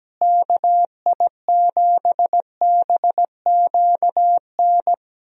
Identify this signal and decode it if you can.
Morse code, heard as a keyed tone, about 17 wpm, 700 Hz. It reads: KI7BQN
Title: USA Call Signs - 6 Characters - Repeated 3x - 17wpm